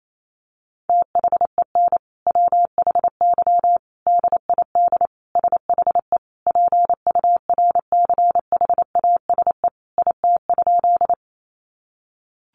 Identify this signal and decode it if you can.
Morse code, heard as a keyed tone, about 28 wpm, 700 Hz. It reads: THEN WHY DID SHE PURCHASE IT?
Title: Sentences from Top 800 Words 28wpm